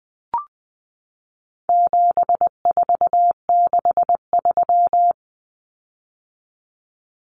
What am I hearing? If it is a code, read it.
Morse code, heard as a keyed tone, about 20 wpm, 700 Hz. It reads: 7463
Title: Common Cut Numbers - 4-Digit 20wpm